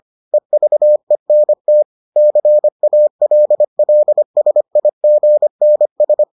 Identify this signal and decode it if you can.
Morse code, heard as a keyed tone, about 25 wpm, 600 Hz. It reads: EVENT CALLSIGNS